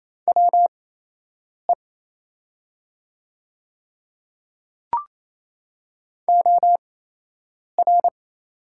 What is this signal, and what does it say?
Morse code, heard as a keyed tone, about 27 wpm, spaced as for 8 wpm, 700 Hz. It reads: WE OR
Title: 8x Character Spacing - Top 100 Words - 28wpm